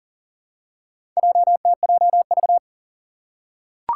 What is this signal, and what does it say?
Morse code, heard as a keyed tone, about 37 wpm, 700 Hz. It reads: JTJV
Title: Four Challenging Letter-Number Combinations 40wpm